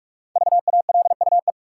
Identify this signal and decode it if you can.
Morse code, heard as a keyed tone, about 42 wpm, 700 Hz. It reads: VALUE
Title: Sentences from Top 600 Words - New Word Drill - 45wpm